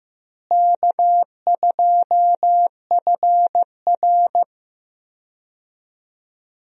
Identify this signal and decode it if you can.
Morse code, heard as a keyed tone, about 15 wpm, 700 Hz. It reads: K2FR
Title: USA Call Signs - 4 Characters - Repeated 2x - 15wpm